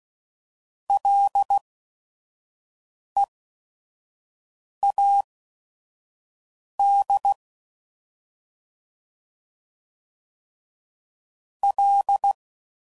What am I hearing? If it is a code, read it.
Morse code, heard as a keyed tone, about 16 wpm, spaced as for 5 wpm, 800 Hz. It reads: LEAD L
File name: Prefixes / FLs & THs / Single Fs Ls